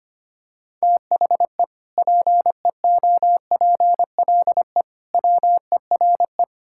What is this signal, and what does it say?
Morse code, heard as a keyed tone, about 25 wpm, 700 Hz. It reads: THE PEOPLE WERE
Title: Sentences from Top 1600 Words 25wpm